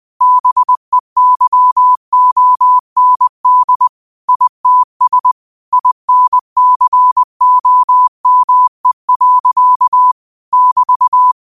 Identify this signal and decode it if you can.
Morse code, heard as a keyed tone, about 20 wpm, 1000 Hz. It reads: BEYOND ITS INCOME. <BT>
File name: qotdfiles / 20wpm